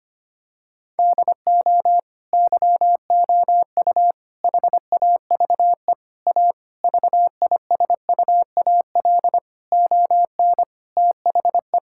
Morse code, heard as a keyed tone, about 25 wpm, 700 Hz. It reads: DO YOU HAVE A VISUAL ON THE
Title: Sentences from Top 1000 Words - Review Part 6 - 25wpm